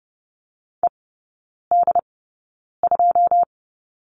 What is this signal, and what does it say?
Morse code, heard as a keyed tone, about 30 wpm, spaced as for 9 wpm, 700 Hz. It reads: ED2
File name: course / 30:10wpm / 087 Lesson 15e - Number 5 - Three Characters 30wpm